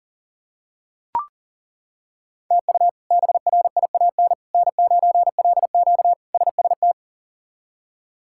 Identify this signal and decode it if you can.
Morse code, heard as a keyed tone, about 40 wpm, 700 Hz. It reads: TU BRIAN N9LQ SST
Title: K1USN SST QSO Practice - 40wpm